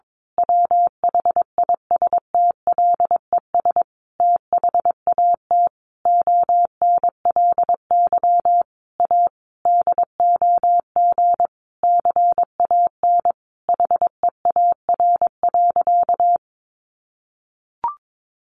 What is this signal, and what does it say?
Morse code, heard as a keyed tone, about 22 wpm, 700 Hz. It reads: WHISTLES THAT ONLY A DOG CAN HEAR.
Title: A Sound Of Thunder 22wpm